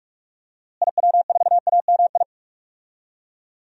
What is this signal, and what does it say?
Morse code, heard as a keyed tone, about 46 wpm, 700 Hz. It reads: IW4AMI
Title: DXCC-100 Call Signs - 6 Characters - Repeated 3x - 45wpm